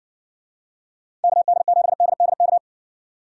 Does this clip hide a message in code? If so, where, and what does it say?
Yes: Morse code from 1.2 to 2.6 s, heard as a keyed tone, about 55 wpm, 700 Hz. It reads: KD7DDK